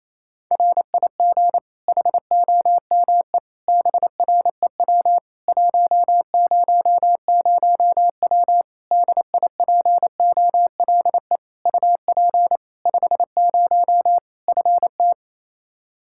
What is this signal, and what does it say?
Morse code, heard as a keyed tone, about 28 wpm, 700 Hz. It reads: RIG HOME BREW 100W DIPOLE UP 50 FT
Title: Practice QSO - Exchange 2 - Info - 28wpm